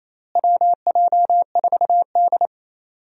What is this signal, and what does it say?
Morse code, heard as a keyed tone, about 28 wpm, 700 Hz. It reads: WJ4D
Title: USA Call Signs - 4 Characters - Repeated 3x - 28wpm